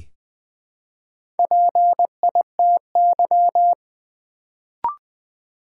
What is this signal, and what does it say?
Morse code, heard as a keyed tone, about 20 wpm, 700 Hz. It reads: PITY
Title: Build A Word - 1 Letter At A Time - 3 to 5-Letter Words - 20wpm